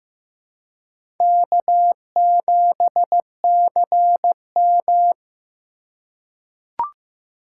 Morse code, heard as a keyed tone, about 15 wpm, 700 Hz. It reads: K7CM
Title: USA Call Signs - 4 Characters - 3x Character Spacing - 15wpm